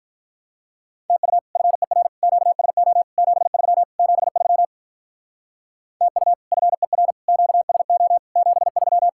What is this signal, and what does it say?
Morse code, heard as a keyed tone, about 45 wpm, 700 Hz. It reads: TU FER QSO 73 73 TU FER QSO 73